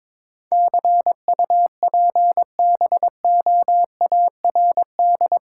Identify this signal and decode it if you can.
Morse code, heard as a keyed tone, about 22 wpm, 700 Hz. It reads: CUPBOARD
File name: practice / 22wpm / Eight Letter Words - 22wpm